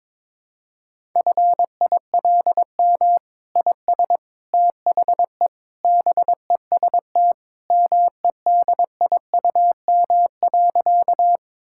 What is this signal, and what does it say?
Morse code, heard as a keyed tone, about 22 wpm, 700 Hz. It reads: FILM IS THE BEST MEDIUM.
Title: Sentences from Top 1000 Words - Review Part 6 - 22wpm